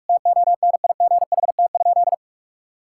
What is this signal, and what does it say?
Morse code, heard as a keyed tone, about 46 wpm, 700 Hz. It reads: TONIGHT?
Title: Sentences from Top 1000 Words - Review Part 5 - 45wpm